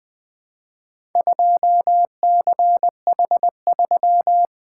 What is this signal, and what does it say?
Morse code, heard as a keyed tone, about 20 wpm, 700 Hz. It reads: 2CH3
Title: Four Letter-Number Combinations - 20wpm